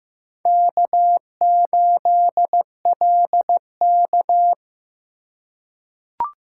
Morse code, heard as a keyed tone, about 15 wpm, 700 Hz. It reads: K8LK
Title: USA Call Signs - 4 Characters - 6x Character Spacing - 15wpm